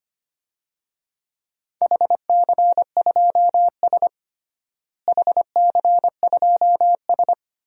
Morse code, heard as a keyed tone, about 25 wpm, 700 Hz. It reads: HC2S HC2S
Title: DXCC-100 Call Signs - 4 Characters - Repeated 3x - 25wpm